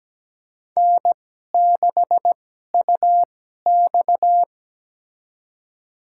Morse code, heard as a keyed tone, about 17 wpm, spaced as for 12 wpm, 700 Hz. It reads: N6UX
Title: USA Call Signs - 4 Characters - 2x Character Spacing - 17wpm